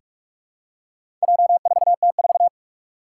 Morse code, heard as a keyed tone, about 46 wpm, 700 Hz. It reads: J4T4
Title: Four Challenging Letter-Number Combinations 45wpm